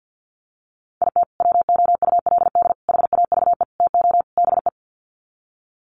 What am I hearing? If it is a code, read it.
Morse code, heard as a keyed tone, about 50 wpm, 700 Hz. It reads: IT WOULD HAVE TO BE
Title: Sets of 5 Words Encore 50wpm